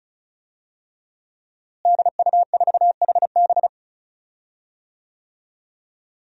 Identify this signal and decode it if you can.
Morse code, heard as a keyed tone, about 36 wpm, 700 Hz. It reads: DU4HB
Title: DXCC-100 Call Signs - 5 Characters - Repeated 3x - 35wpm